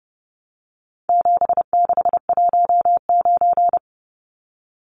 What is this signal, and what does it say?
Morse code, heard as a keyed tone, about 30 wpm, 700 Hz. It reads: 7619
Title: Numbers - 4-Digit 30wpm